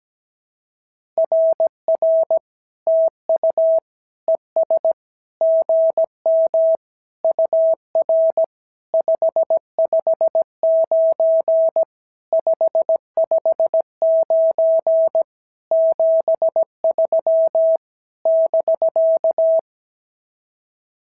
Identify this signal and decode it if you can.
Morse code, heard as a keyed tone, about 17 wpm, 650 Hz. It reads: RR TU ES GM UR 559 559 73 <BK>
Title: SOTA QSO Practice 17wpm